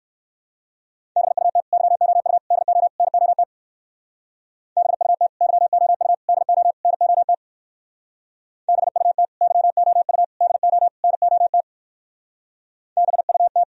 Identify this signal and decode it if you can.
Morse code, heard as a keyed tone, about 58 wpm, 700 Hz. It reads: BUT YOU DO NOT BUT YOU DO NOT BUT YOU DO NOT BUT